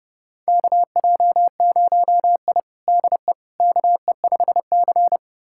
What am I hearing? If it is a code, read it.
Morse code, heard as a keyed tone, about 31 wpm, 700 Hz. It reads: KJ0I DE KE5C
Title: USA Callsign DE Callsign - 4 Characters - 30wpm